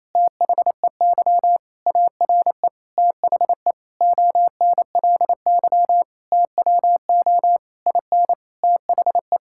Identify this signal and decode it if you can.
Morse code, heard as a keyed tone, about 27 wpm, 700 Hz. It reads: THEY ARE THE ONLY TWO IN THE
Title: Sentences from Top 1000 Words - Review Part 6 - 28wpm